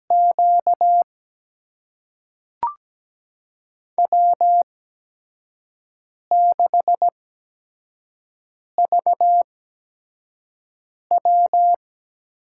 Morse code, heard as a keyed tone, about 17 wpm, spaced as for 5 wpm, 700 Hz. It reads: Q W6VW
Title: USA Call Signs - 8x Character Spacing - 17wpm